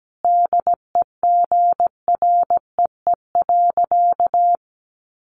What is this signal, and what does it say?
Morse code, heard as a keyed tone, about 17 wpm, 700 Hz. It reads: DEGREE.
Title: Sentences from Top 1500 Words 17wpm